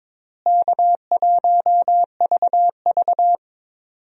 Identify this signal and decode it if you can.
Morse code, heard as a keyed tone, about 22 wpm, 700 Hz. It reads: K1VV
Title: USA Call Signs - 4 Characters - Repeated 3x - 22wpm